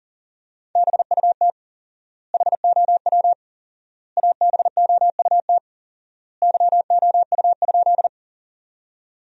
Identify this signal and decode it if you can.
Morse code, heard as a keyed tone, about 37 wpm, 700 Hz. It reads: BUT HOW ABOUT YOU?
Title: 4x Word-Spacing - Sentences from Top 100 Words - 40wpm